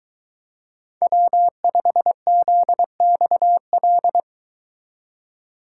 Morse code, heard as a keyed tone, about 23 wpm, 700 Hz. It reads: W5ZXL